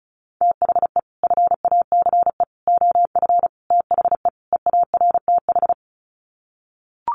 Morse code, heard as a keyed tone, about 34 wpm, 700 Hz. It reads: THE FACE OF THE EARTH
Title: Sets of 5 Words Encore 35wpm